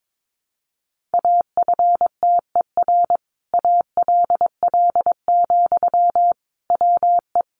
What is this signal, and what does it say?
Morse code, heard as a keyed tone, about 22 wpm, 700 Hz. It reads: AFTER ALL, WE
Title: Sentences from Top 1300 Words 22wpm